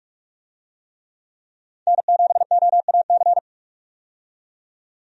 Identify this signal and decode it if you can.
Morse code, heard as a keyed tone, about 46 wpm, 700 Hz. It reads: N7OAC
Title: USA Call Signs - 5 Characters - 45wpm